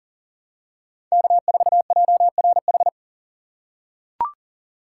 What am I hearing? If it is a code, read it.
Morse code, heard as a keyed tone, about 40 wpm, 700 Hz. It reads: K4JRH
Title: USA Call Signs - 4x Character Spacing - 40wpm